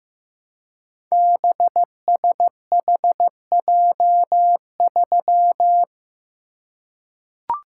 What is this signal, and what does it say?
Morse code, heard as a keyed tone, about 15 wpm, 700 Hz. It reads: BSHJ3
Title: Code Groups - 5 Letter:Numbers - 15wpm